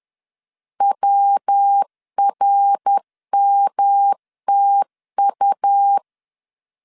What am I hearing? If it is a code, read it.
Morse code, heard as a keyed tone, about 10 wpm, 800 Hz. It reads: WRMTU
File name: CW / DET / 1 3 1500C S C 10wpm 3335